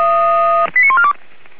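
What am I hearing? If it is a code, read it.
Five-tone selective call (ZVEI), heard as quick runs of five tones, tones of about 70 ms. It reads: 89142
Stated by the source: ZVEI five-tone selective call received by SDR.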